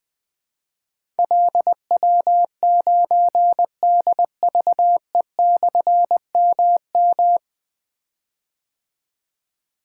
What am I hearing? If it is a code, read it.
Morse code, heard as a keyed tone, about 21 wpm, 700 Hz. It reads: LW9DVE/MM
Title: Difficult Call Signs 20wpm